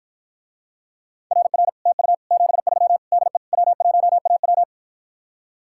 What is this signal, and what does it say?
Morse code, heard as a keyed tone, about 53 wpm, 700 Hz. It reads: RR TU 73 DE W1AW